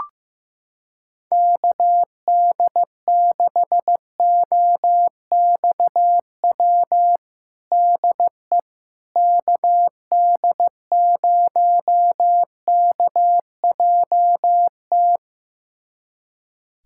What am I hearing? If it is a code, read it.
Morse code, heard as a keyed tone, about 15 wpm, 700 Hz. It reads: KD6OXW DE KD0KJT
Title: USA Callsign DE Callsign - 6 Characters - 15wpm